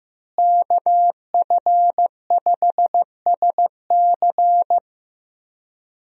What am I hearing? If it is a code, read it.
Morse code, heard as a keyed tone, about 16 wpm, 700 Hz. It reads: KF5SC
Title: USA Call Signs - 5 Characters - Repeated 2x - 15wpm